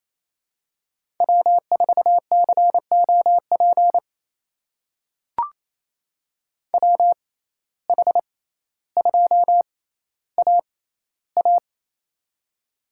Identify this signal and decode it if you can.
Morse code, heard as a keyed tone, about 28 wpm, spaced as for 14 wpm, 700 Hz. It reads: W4COP WH2AA
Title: USA Call Signs - 6x Character Spacing - 28wpm